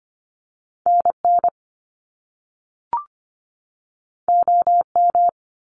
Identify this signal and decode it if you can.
Morse code, heard as a keyed tone, about 25 wpm, 700 Hz. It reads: NN OM
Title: Top Two Letter Combinations in Words 25wpm